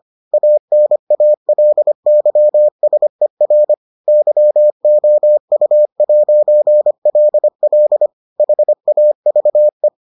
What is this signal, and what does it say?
Morse code, heard as a keyed tone, about 25 wpm, 600 Hz. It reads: ANALYSER YOU'LL HAVE